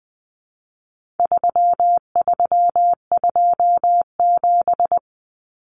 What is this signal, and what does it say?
Morse code, heard as a keyed tone, about 20 wpm, 700 Hz. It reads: 3327
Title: Common Cut Numbers - 4-Digit 20wpm